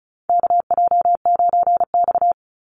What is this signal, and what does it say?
Morse code, heard as a keyed tone, about 34 wpm, 700 Hz. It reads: KJ9X
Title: NA SOTA Chaser Callsigns 35wpm